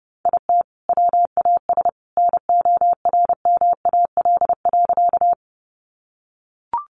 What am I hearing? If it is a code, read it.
Morse code, heard as a keyed tone, about 28 wpm, 700 Hz. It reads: IT WAS NORMAL.
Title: Sentences from Top 1200 Words 30wpm